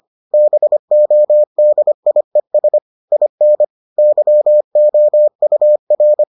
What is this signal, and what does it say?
Morse code, heard as a keyed tone, about 25 wpm, 600 Hz. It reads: BODIES IN YOUR